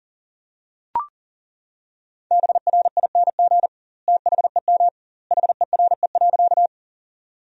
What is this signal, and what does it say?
Morse code, heard as a keyed tone, about 40 wpm, 700 Hz. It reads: BRING THEM HERE.
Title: 2x Word-Spacing - Sentences from Top 500 Words - 40wpm